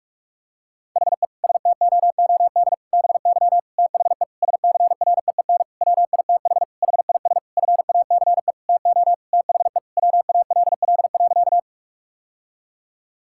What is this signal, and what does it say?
Morse code, heard as a keyed tone, about 46 wpm, 700 Hz. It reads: HE STOOD BY THE SCREEN WITH HIS FACE TO THE WALL.
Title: Sentences from Top 1000 Words - Review Part 6 - 45wpm